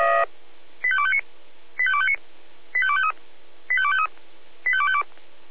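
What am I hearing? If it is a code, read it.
Five-tone selective call (ZVEI), heard as quick runs of five tones, tones of about 70 ms. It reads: 85259 85259 85253 85253 85252
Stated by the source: ZVEI five-tone selective call received on SDR.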